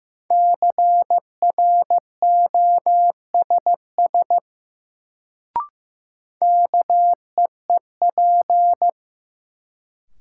Morse code, heard as a keyed tone, about 15 wpm, 700 Hz. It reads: CROSS KEEP
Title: Sentences from Top 300 Words - New Word Drill - 15wpm